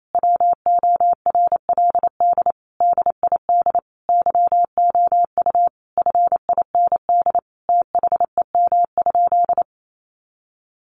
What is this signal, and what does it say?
Morse code, heard as a keyed tone, about 28 wpm, 700 Hz. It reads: WORLD DID YOU FIND THEM?
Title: Sentences from Top 200 Words 28wpm